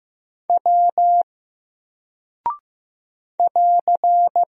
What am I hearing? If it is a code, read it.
Morse code, heard as a keyed tone, about 15 wpm, 700 Hz. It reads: W <AR>
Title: Single Character (Letter:Number:Punctuation:Prosign) 15wpm